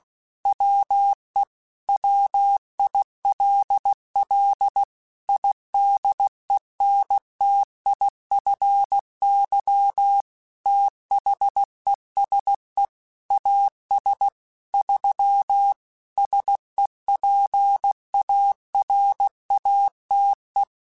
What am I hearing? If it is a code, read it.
Morse code, heard as a keyed tone, about 16 wpm, 800 Hz. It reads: WE WILL IDENTIFY THESE AS 3 SEPARATE
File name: Technical Subjects / Technical Subjects-mp3s / Sw Pwr Sup @15